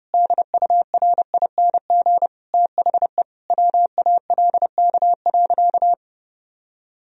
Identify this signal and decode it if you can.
Morse code, heard as a keyed tone, about 32 wpm, 700 Hz. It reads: DURING THE WALK.
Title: Sentences from Top 1000 Words - Review Part 2 - 30wpm